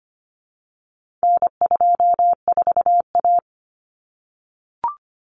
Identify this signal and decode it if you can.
Morse code, heard as a keyed tone, about 25 wpm, 700 Hz. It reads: N24A
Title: Common Cut Numbers - 4-Digit 25wpm